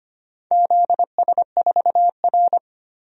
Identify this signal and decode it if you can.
Morse code, heard as a keyed tone, about 25 wpm, 700 Hz. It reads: ZS4R
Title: DXCC-100 Call Signs - 4 Characters - 25wpm